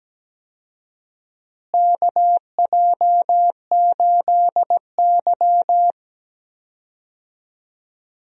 Morse code, heard as a keyed tone, about 17 wpm, 700 Hz. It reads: KJ8Y
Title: USA Call Signs - 4 Characters - Repeated 2x - 17wpm